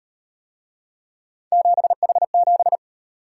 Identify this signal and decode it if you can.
Morse code, heard as a keyed tone, about 38 wpm, 700 Hz. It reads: 7H7